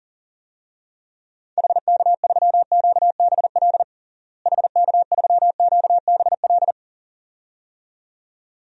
Morse code, heard as a keyed tone, about 40 wpm, 700 Hz. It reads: HK3QBL HK3QBL
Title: DXCC-100 Call Signs - 6 Characters - Repeated 2x - 40wpm